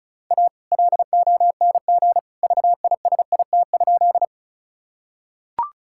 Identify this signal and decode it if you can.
Morse code, heard as a keyed tone, about 36 wpm, 700 Hz. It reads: A LONG VISIT?
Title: Sentences from Top 1000 Words - Review Part 6 - 35wpm